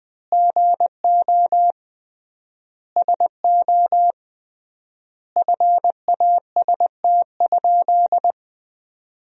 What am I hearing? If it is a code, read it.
Morse code, heard as a keyed tone, about 21 wpm, 700 Hz. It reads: GO SO FAST?
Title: 3x Word-Spacing - Sentences from Top 500 Words - 20wpm